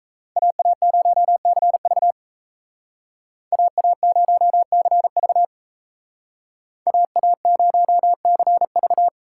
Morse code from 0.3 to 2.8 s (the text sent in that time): AA0CV